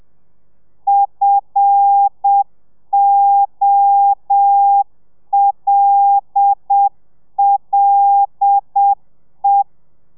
Morse code, heard as a keyed tone, about 7 wpm, 800 Hz. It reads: FOLLE
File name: telegrafia / i2tfj / Lez5 30